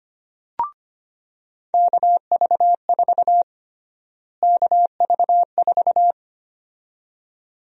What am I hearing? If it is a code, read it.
Morse code, heard as a keyed tone, about 25 wpm, 700 Hz. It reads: KV4 KV4
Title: USA Call Sign Prefixes - Repeated 2x - 25wpm